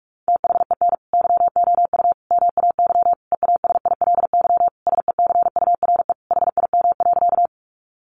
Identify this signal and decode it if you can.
Morse code, heard as a keyed tone, about 44 wpm, 700 Hz. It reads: THEN YOU MAY EASILY SECURE HIM.
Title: Sentences from Top 1200 Words 45wpm